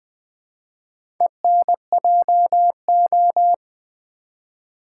Morse code, heard as a keyed tone, about 20 wpm, 700 Hz. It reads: ENJO
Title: Build A Word - 1 Letter At A Time - 5-Letter Words - 20wpm